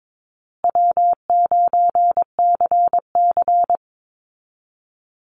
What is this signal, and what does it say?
Morse code, heard as a keyed tone, about 22 wpm, 700 Hz. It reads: W9CC
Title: Call Signs Repeated 2x - 22wpm